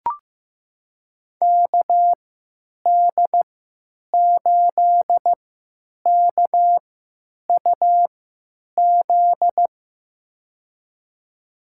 Morse code, heard as a keyed tone, about 15 wpm, spaced as for 9 wpm, 700 Hz. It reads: KD8KUZ
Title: USA Call Signs - 6 Characters - 3x Character Spacing - 15wpm